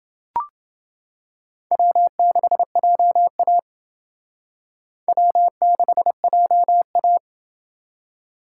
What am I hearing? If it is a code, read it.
Morse code, heard as a keyed tone, about 28 wpm, 700 Hz. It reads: W6JA W6JA